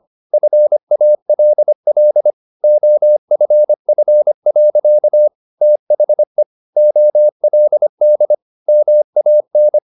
Morse code, heard as a keyed tone, about 25 wpm, 600 Hz. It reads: FALL OFF. THE OLD MAN